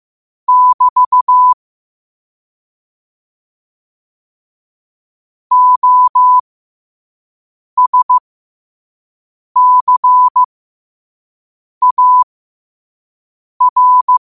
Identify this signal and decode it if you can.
Morse code, heard as a keyed tone, about 15 wpm, spaced as for 5 wpm, 1000 Hz. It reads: <BT> OSCAR
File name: qotdfiles / 5wpm